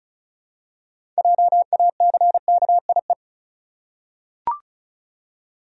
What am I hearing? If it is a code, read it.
Morse code, heard as a keyed tone, about 36 wpm, 700 Hz. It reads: JACKIE
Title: USA Common Names - 35wpm